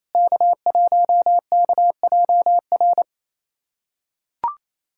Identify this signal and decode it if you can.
Morse code, heard as a keyed tone, about 28 wpm, 700 Hz. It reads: K1KJR
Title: USA Callsign DE Callsign 28wpm